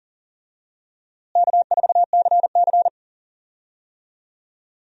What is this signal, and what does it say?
Morse code, heard as a keyed tone, about 40 wpm, 700 Hz. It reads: K4CC
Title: USA Call Signs - 4 Characters - Repeated 3x - 40wpm